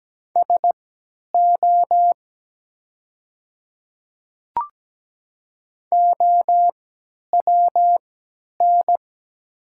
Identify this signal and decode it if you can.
Morse code, heard as a keyed tone, about 17 wpm, spaced as for 10 wpm, 700 Hz. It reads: SO OWN
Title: 3x Character Spacing - Top 300 Words - 17wpm